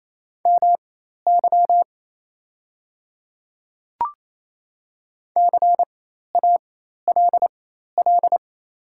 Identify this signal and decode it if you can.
Morse code, heard as a keyed tone, about 28 wpm, spaced as for 13 wpm, 700 Hz. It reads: MY CALL
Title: 4x Character Spacing - Top 100 Words - 28wpm